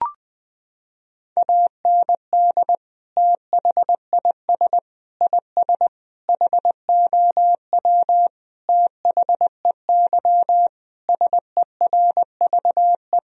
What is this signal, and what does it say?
Morse code, heard as a keyed tone, about 20 wpm, 700 Hz. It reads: AND THIS IS HOW THEY SERVE
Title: Sentences from Top 1000 Words - Review Part 2 - 20wpm